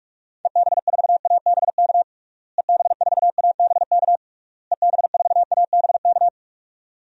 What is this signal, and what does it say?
Morse code, heard as a keyed tone, about 46 wpm, 700 Hz. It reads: EB4ABK EB4ABK EB4ABK
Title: DXCC-100 Call Signs - 6 Characters - Repeated 3x - 45wpm